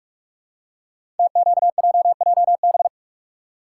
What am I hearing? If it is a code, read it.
Morse code, heard as a keyed tone, about 44 wpm, 700 Hz. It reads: TQJJB
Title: Code Groups - 5 Letter:Numbers - 45wpm